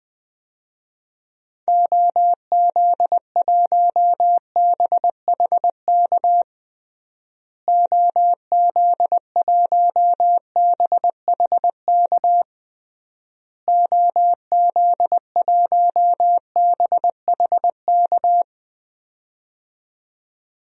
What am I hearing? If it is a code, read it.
Morse code, heard as a keyed tone, about 21 wpm, 700 Hz. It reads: OZ1BHK OZ1BHK OZ1BHK
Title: DXCC-100 Call Signs - 6 Characters - Repeated 3x - 20wpm